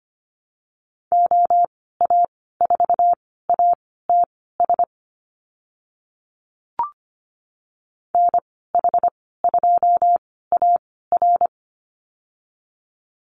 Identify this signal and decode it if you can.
Morse code, heard as a keyed tone, about 25 wpm, spaced as for 16 wpm, 700 Hz. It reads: OA4ATS NH2AR
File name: Lesson 15g - Number 5 - Call Signs 25wpm